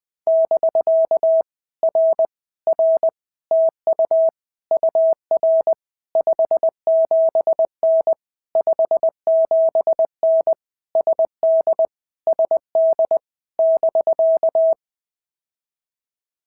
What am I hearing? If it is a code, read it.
Morse code, heard as a keyed tone, about 20 wpm, 650 Hz. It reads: <BK> R R TU UR 57N 57N SD SD <BK>
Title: POTA QSO Practice - 20wpm